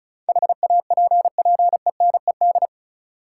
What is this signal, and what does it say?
Morse code, heard as a keyed tone, about 36 wpm, 700 Hz. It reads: HAPPENED
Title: Sentences from Top 1000 Words - Review Part 2 - New Word Drill - 35wpm